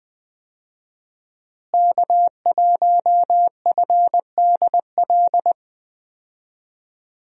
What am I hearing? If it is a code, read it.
Morse code, heard as a keyed tone, about 21 wpm, 700 Hz. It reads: K1FDL
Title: USA Call Signs - 5 Characters - Repeated 2x - 20wpm